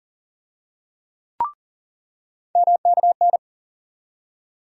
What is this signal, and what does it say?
Morse code, heard as a keyed tone, about 40 wpm, 700 Hz. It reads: MKN
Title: USA Call Sign Postfixes - 3 Letters 40wpm